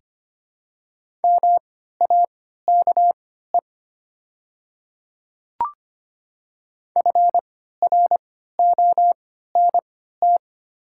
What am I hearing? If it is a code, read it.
Morse code, heard as a keyed tone, about 25 wpm, spaced as for 14 wpm, 700 Hz. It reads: MAKE FRONT
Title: 3x Character Spacing - Top 500 Words - 25wpm